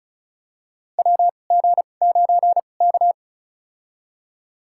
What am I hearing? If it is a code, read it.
Morse code, heard as a keyed tone, about 36 wpm, spaced as for 25 wpm, 700 Hz. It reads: WG9K
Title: USA Call Signs - 4 Characters - 2x Character Spacing - 35wpm